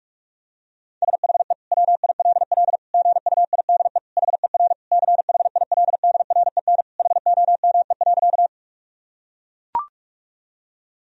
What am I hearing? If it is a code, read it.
Morse code, heard as a keyed tone, about 43 wpm, 700 Hz. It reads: SHE WILL GUIDE HER CHILDREN HOME.
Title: Sentences from Top 1000 Words - Review Part 3 - 45wpm